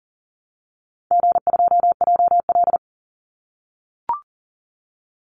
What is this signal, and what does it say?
Morse code, heard as a keyed tone, about 40 wpm, 700 Hz. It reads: G2JL
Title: Easy Call Signs - 40wpm